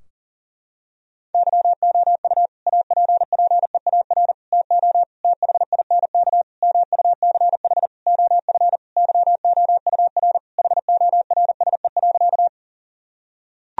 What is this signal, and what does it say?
Morse code, heard as a keyed tone, about 40 wpm, 700 Hz. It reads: YOU APPEAR TO THINK MUCH OF YOUR HORSE.
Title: Sentences from Top 1000 Words - Review Part 2 - 40wpm